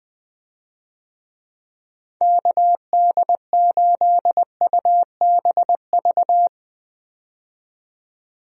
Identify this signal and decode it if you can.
Morse code, heard as a keyed tone, about 20 wpm, 700 Hz. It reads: KD8UBV
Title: USA Call Signs - 6 Characters - 20wpm